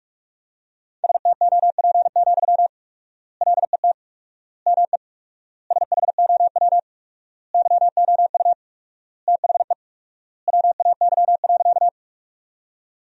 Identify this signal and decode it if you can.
Morse code, heard as a keyed tone, about 45 wpm, 700 Hz. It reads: STOP, LET ME SHOW YOU THE WAY.
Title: 4x Word-Spacing - Sentences from Top 300 Words - 45wpm